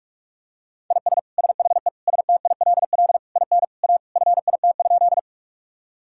Morse code, heard as a keyed tone, about 46 wpm, 700 Hz. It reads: IS SHE STILL IN A FIT?
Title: Sentences from Top 1000 Words - Review Part 5 - 45wpm